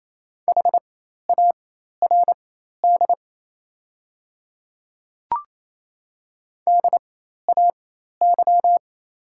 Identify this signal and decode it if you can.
Morse code, heard as a keyed tone, about 27 wpm, spaced as for 13 wpm, 700 Hz. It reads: HARD DAY
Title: 4x Character Spacing - Top 300 Words - 28wpm